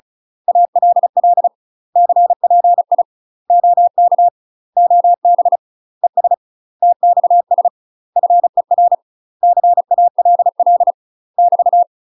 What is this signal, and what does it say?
Morse code, heard as a keyed tone, about 34 wpm, 700 Hz. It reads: ALL CPI OK OB ES TXS FER CALL <BT>
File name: UK MASTER QSOs TRAINING / 35WPM / UK MASTER QSOs 0041